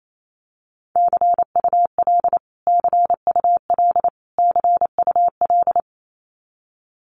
Morse code, heard as a keyed tone, about 28 wpm, 700 Hz. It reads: CUL CUL CUL
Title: Abbreviations Repeated 3x - 28wpm